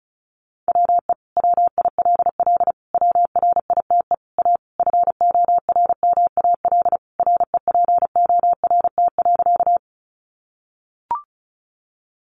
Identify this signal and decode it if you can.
Morse code, heard as a keyed tone, about 34 wpm, 700 Hz. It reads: WE WILL WRITE A FORMAL REPORT.
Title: Sentences from Top 1300 Words 35wpm